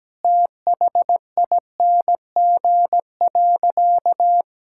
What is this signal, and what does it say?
Morse code, heard as a keyed tone, about 17 wpm, 700 Hz. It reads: THING.
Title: Sentences from Top 1000 Words - Review Part 5 - 17wpm